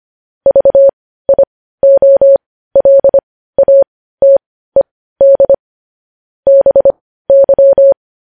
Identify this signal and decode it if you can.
Morse code, heard as a keyed tone, about 25 wpm, spaced as for 15 wpm, 550 Hz. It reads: VIOLATED BY
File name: podcast / News headlines-1x-25x15